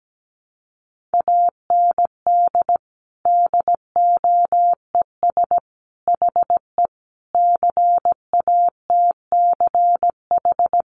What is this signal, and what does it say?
Morse code, heard as a keyed tone, about 17 wpm, 700 Hz. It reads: AND DOES HE CATCH
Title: Sentences from Top 1500 Words 17wpm